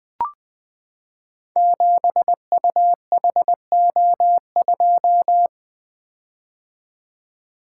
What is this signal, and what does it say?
Morse code, heard as a keyed tone, about 20 wpm, 700 Hz. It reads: 7UHO2
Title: Code Groups - 5 Letter:Numbers - 20wpm